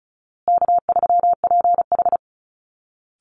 Morse code, heard as a keyed tone, about 34 wpm, 700 Hz. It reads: K3PH
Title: Easy Call Signs Repeated 3x - 35wpm